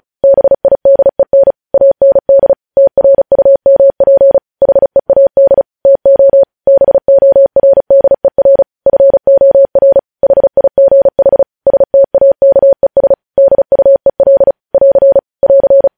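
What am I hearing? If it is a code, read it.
Morse code, heard as a keyed tone, about 36 wpm, 550 Hz. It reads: BIDEN AND TRUMP HEAD TO BORDER FOR HIGH STAKES DUEL <AR> <AR>
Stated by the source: Morse code news headlines sent at 35 wpm.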